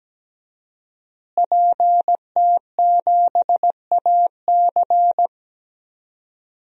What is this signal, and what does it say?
Morse code, heard as a keyed tone, about 17 wpm, 700 Hz. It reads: PT7AC
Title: DXCC-100 Call Signs - 5 Characters - Repeated 3x - 17wpm